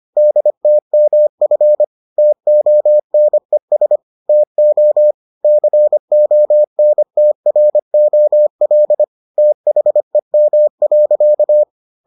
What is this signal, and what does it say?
Morse code, heard as a keyed tone, about 25 wpm, 600 Hz. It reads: DTMF TONES TO CONTROL THEM.